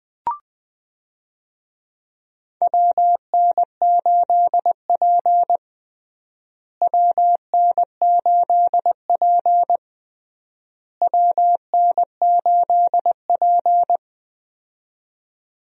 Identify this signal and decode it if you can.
Morse code, heard as a keyed tone, about 20 wpm, 700 Hz. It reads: WN8P WN8P WN8P
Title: USA Call Signs - 4 Characters - Repeated 3x - 20wpm